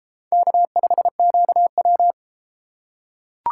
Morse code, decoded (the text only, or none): K5QW